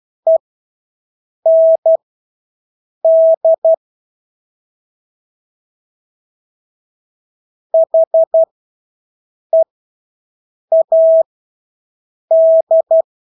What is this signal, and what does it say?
Morse code, heard as a keyed tone, about 12 wpm, spaced as for 6 wpm, 650 Hz. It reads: END HEAD
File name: Engelske ord 12-6 wpm